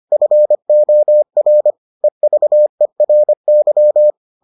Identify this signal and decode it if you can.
Morse code, heard as a keyed tone, about 25 wpm, 600 Hz. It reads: FOR EVERY